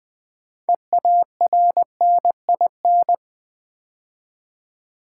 Morse code, heard as a keyed tone, about 20 wpm, 700 Hz. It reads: EARNIN
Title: Build A Word - 2 Letters At A Time - 8-Letter Words - 20wpm